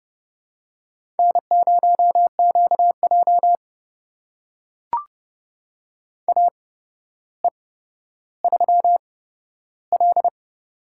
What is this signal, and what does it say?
Morse code, heard as a keyed tone, about 29 wpm, spaced as for 13 wpm, 700 Hz. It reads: N0QJ AE3L
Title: USA Call Signs - 4 Characters - 8x Character Spacing - 30wpm